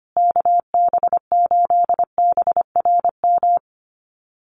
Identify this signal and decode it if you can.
Morse code, heard as a keyed tone, about 25 wpm, 700 Hz. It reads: KB8BRM
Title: USA Call Signs Repeated 2x 25wpm